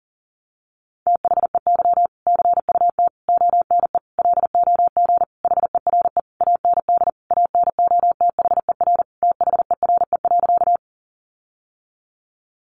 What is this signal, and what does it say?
Morse code, heard as a keyed tone, about 40 wpm, 700 Hz. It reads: THEY CUT ONE LOG HERE AND ANOTHER THERE.
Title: Sentences from Top 1400 Words 40wpm